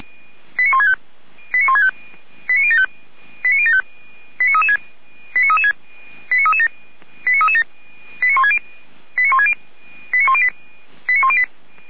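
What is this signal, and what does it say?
Five-tone selective call (ZVEI), heard as quick runs of five tones, tones of about 70 ms. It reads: 89165 89165 89075 89075 89336 89336 89337 89337 89160 89108 89108
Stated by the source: ZVEI five-tone selective call received on an SDR, from the Feuerwehr Orlamünde fire brigade audio archive.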